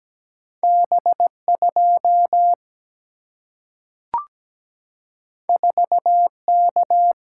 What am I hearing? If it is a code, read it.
Morse code, heard as a keyed tone, about 17 wpm, 700 Hz. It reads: B2 4K
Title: Single-Letter Single-Number 17wpm